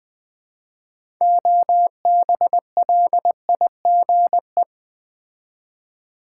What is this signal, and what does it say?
Morse code, heard as a keyed tone, about 20 wpm, 700 Hz. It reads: OBLIGE
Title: 6 Letter Words - First 3 Letters Given - 20wpm